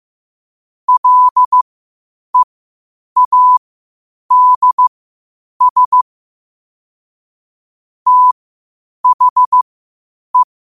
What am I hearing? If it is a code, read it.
Morse code, heard as a keyed tone, about 15 wpm, spaced as for 8 wpm, 1000 Hz. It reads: LEADS THE
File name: qotdfiles / 8wpm